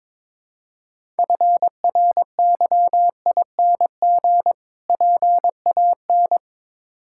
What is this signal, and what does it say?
Morse code, heard as a keyed tone, about 22 wpm, 700 Hz. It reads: FRYING PAN
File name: practice / 22wpm / Difficult Words with Hints - 22wpm